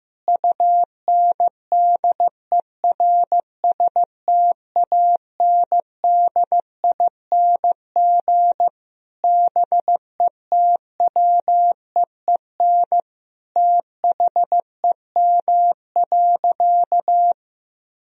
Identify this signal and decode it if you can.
Morse code, heard as a keyed tone, about 16 wpm, 700 Hz. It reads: UNDERSTANDING BETWEEN THEM.
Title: Sentences from Top 1000 Words - Review Part 3 - 15wpm